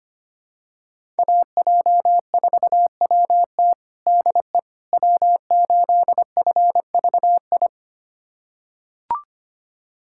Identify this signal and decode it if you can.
Morse code, heard as a keyed tone, about 25 wpm, 700 Hz. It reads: AJ4WT DE W8FVI
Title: USA Callsign DE Callsign - 5 Characters - 25wpm